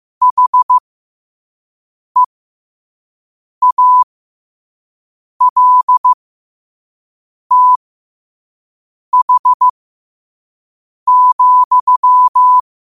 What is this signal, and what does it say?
Morse code, heard as a keyed tone, about 15 wpm, spaced as for 5 wpm, 1000 Hz. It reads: HEALTH,
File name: qotdfiles / 5wpm